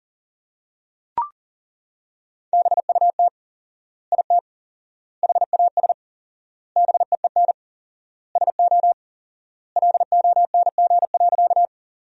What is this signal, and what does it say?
Morse code, heard as a keyed tone, about 40 wpm, 700 Hz. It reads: BUT IT HAS BEEN SO LONG.
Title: 4x Word-Spacing - Sentences from Top 100 Words - 40wpm